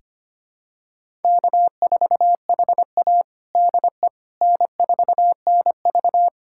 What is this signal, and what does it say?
Morse code, heard as a keyed tone, about 25 wpm, 700 Hz. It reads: K4HA DE N4NV
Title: USA Callsign DE Callsign - 4 Characters - 25wpm